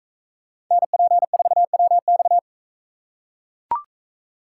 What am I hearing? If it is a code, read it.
Morse code, heard as a keyed tone, about 41 wpm, 700 Hz. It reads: NP4WX